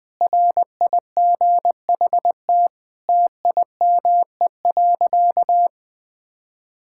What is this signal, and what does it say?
Morse code, heard as a keyed tone, about 20 wpm, 700 Hz. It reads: RIGHT TIME.
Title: Sentences with 8-Word Phrases - 20wpm